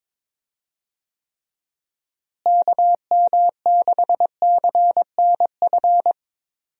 Morse code, heard as a keyed tone, about 21 wpm, 700 Hz. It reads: KM6CNF
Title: USA Call Signs - 6 Characters - Repeated 2x - 22wpm